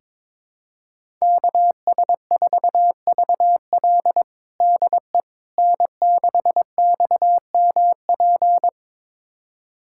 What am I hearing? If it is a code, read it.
Morse code, heard as a keyed tone, about 22 wpm, 700 Hz. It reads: KS4VL DE N6XMP
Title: USA Callsign DE Callsign - 5 Characters - 22wpm